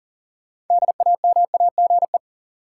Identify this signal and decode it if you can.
Morse code, heard as a keyed tone, about 37 wpm, 700 Hz. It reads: DAMAGE
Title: Sentences from Top 1100 Words - New Word Drill - 40wpm